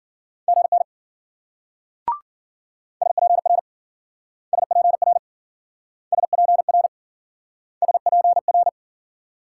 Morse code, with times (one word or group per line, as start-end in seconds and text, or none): 0.5-0.8 GN
3.0-3.6 SPR
4.5-5.2 SPR
6.1-6.9 SPR
7.8-8.7 SPR